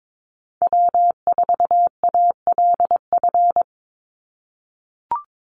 Morse code, heard as a keyed tone, about 22 wpm, 700 Hz. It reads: W4ALF
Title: Call Signs - 22wpm